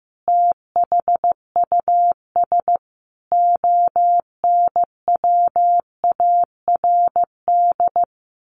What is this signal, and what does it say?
Morse code, heard as a keyed tone, about 15 wpm, 700 Hz. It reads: THUS ONWARD
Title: A Sound Of Thunder 15wpm